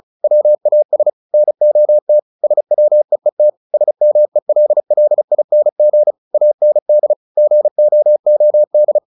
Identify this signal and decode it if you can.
Morse code, heard as a keyed tone, about 34 wpm, 600 Hz. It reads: WAS NOT SWEET SMELLING AND GOOD
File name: the old man and the sea / the old man and the sea 1 35wpm